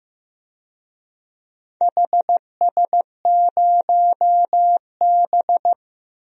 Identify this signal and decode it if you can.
Morse code, heard as a keyed tone, about 15 wpm, 700 Hz. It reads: HS0B
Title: DXCC-100 Call Signs - 4 Characters - 15wpm